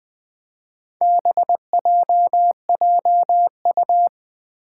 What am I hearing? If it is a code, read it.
Morse code, heard as a keyed tone, about 20 wpm, 700 Hz. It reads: BJJU
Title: Four Challenging Letter-Number Combinations 20wpm